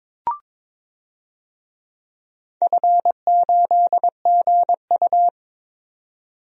Morse code, heard as a keyed tone, about 22 wpm, 700 Hz. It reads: F8GU
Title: DXCC-100 Call Signs - 4 Characters - 22wpm